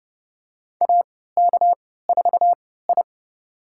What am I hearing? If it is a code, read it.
Morse code, heard as a keyed tone, about 30 wpm, spaced as for 17 wpm, 700 Hz. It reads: AK4I
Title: USA Call Signs - 4 Characters - 3x Character Spacing - 30wpm